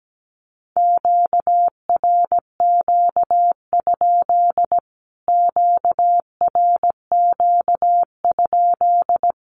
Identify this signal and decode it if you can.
Morse code, heard as a keyed tone, about 17 wpm, 700 Hz. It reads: QRQ? QRQ?
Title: Q-Codes Repeated 3x 17wpm